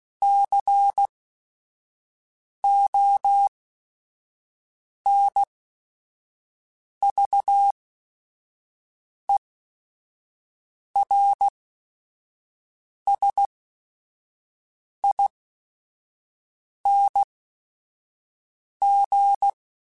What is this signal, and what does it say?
Morse code, heard as a keyed tone, about 16 wpm, spaced as for 5 wpm, 800 Hz. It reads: CONVERSING